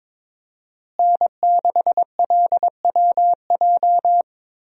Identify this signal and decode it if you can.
Morse code, heard as a keyed tone, about 22 wpm, 700 Hz. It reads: N6LWJ
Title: USA Call Signs - 5 Characters - Repeated 3x - 22wpm